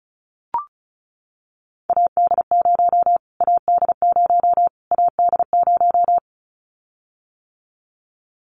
Morse code, heard as a keyed tone, about 34 wpm, 700 Hz. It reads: AD0 AD0 AD0
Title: Callsign Prefixes for Top 25 DXCC - Repeated 3x - 35wpm